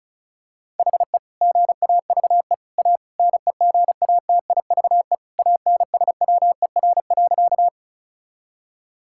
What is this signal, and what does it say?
Morse code, heard as a keyed tone, about 34 wpm, 690 Hz. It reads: HE GAVE A NEGATIVE ANSWER.
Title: Sentences from Top 1000 Words - Review Part 6 - 35wpm